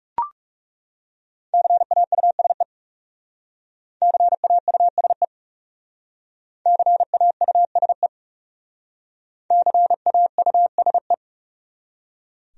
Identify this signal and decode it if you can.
Morse code, heard as a keyed tone, about 38 wpm, 700 Hz. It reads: CAUSE CAUSE CAUSE CAUSE